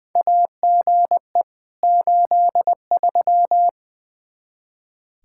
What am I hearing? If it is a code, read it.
Morse code, heard as a keyed tone, about 20 wpm, 700 Hz. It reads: AGE 83
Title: Practice QSO - Exchange 3 - More Info - 20wpm